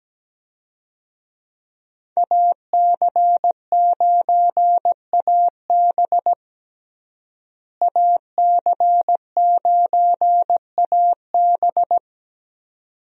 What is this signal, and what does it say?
Morse code, heard as a keyed tone, about 17 wpm, 700 Hz. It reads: AC9AB AC9AB
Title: USA Call Signs - 5 Characters - Repeated 2x - 17wpm